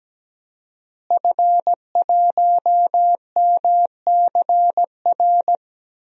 Morse code, heard as a keyed tone, about 17 wpm, 700 Hz. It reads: F1MCR
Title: DXCC-100 Call Signs - 5 Characters - Repeated 2x - 17wpm